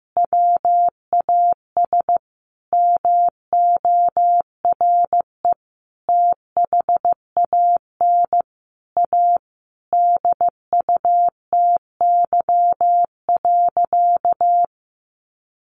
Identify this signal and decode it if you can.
Morse code, heard as a keyed tone, about 15 wpm, 700 Hz. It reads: WAS MORE THAN A DUTY.
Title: Sentences from Top 1500 Words 15wpm